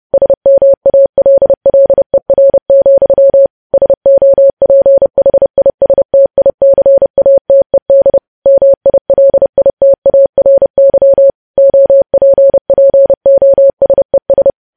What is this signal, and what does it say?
Morse code, heard as a keyed tone, about 30 wpm, 550 Hz. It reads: SMALLER, SOPHISTICATED MILITARY OPPOSES
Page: news headlines in morse code at 30 wpm